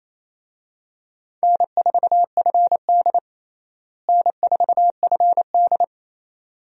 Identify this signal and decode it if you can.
Morse code, heard as a keyed tone, about 28 wpm, 700 Hz. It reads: N4FD N4FD
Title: USA Call Signs - 4 Characters - Repeated 2x - 28wpm